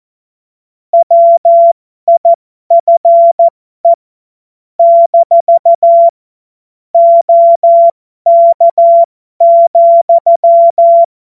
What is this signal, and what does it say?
Morse code, heard as a keyed tone, about 14 wpm, spaced as for 12 wpm, 680 Hz. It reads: WIFE - OK,
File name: Morsekurs / Witze